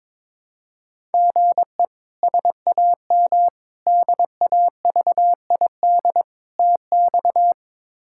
Morse code, heard as a keyed tone, about 22 wpm, 700 Hz. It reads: GE SAM DAVID TX
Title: K1USN SST QSO Practice - 22wpm